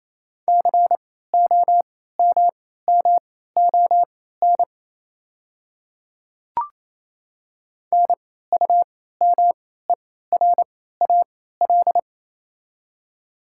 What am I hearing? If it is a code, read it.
Morse code, heard as a keyed tone, about 28 wpm, spaced as for 16 wpm, 700 Hz. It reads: COMMON NUMERAL
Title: 3x Character Spacing - Top 500 Words - 28wpm